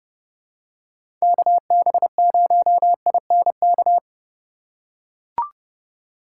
Morse code, heard as a keyed tone, about 29 wpm, 700 Hz. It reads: KB0INK
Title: USA Call Signs - 3x Character Spacing - 30wpm